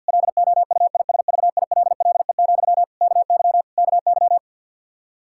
Morse code, heard as a keyed tone, about 46 wpm, 700 Hz. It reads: LOUISVILLE, KY KY
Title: Practice QSO - Exchange 1 - Beginning - 50wpm